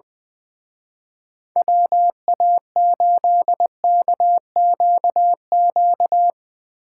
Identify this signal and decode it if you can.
Morse code, heard as a keyed tone, about 22 wpm, 700 Hz. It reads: WA8KQQ
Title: Speed Racing - USA Call Signs 20wpm